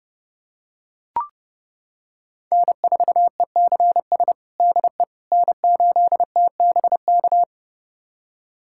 Morse code, heard as a keyed tone, about 30 wpm, 700 Hz. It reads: N4ECS DE N8TBK
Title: USA Callsign DE Callsign - 5 Characters - 30wpm